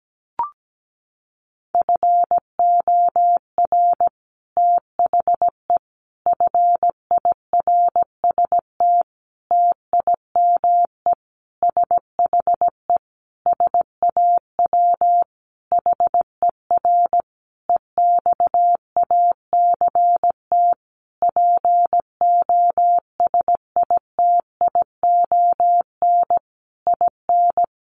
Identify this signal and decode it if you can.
Morse code, heard as a keyed tone, about 17 wpm, 700 Hz. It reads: FOR THE FIRST TIME SHE SAW HER EXACT POSITION IN
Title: Sentences from Top 1300 Words 17wpm